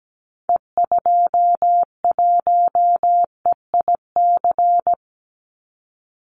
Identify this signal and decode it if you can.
Morse code, heard as a keyed tone, about 17 wpm, 700 Hz. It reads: E21EIC
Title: Call Signs - 17wpm